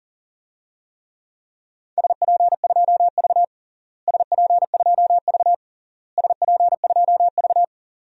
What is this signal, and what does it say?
Morse code, heard as a keyed tone, about 40 wpm, 700 Hz. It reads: SP2V SP2V SP2V
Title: DXCC-100 Call Signs - 4 Characters - Repeated 3x - 40wpm